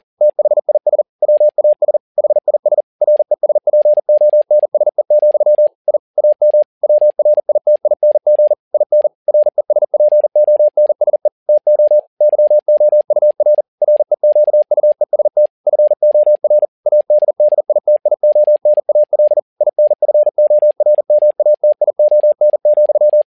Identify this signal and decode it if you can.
Morse code, heard as a keyed tone, about 40 wpm, 600 Hz. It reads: THIS WAS HIS RESPONSE, I AM WRITING IN RESPONSE TO YOUR REQUEST FOR ADDITIONAL INFORMATION,
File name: learning center / tower trouble / tower touble 40wpm